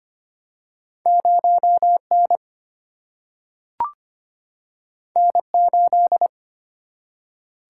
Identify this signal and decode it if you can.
Morse code, heard as a keyed tone, about 25 wpm, 700 Hz. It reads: 0 N N 8
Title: Single-Letter Single-Number 25wpm